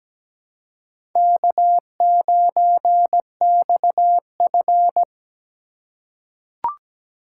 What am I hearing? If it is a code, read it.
Morse code, heard as a keyed tone, about 17 wpm, 700 Hz. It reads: K9XF
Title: USA Call Signs - 6x Character Spacing - 17wpm